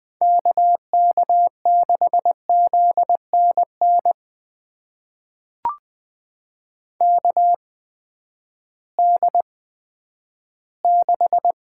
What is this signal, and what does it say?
Morse code, heard as a keyed tone, about 20 wpm, spaced as for 11 wpm, 700 Hz. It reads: KK6ZNN KD6
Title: USA Call Signs - 6 Characters - 8x Character Spacing - 20wpm